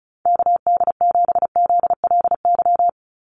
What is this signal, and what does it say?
Morse code, heard as a keyed tone, about 34 wpm, 700 Hz. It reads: KD7ZLY
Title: USA Call Signs 35wpm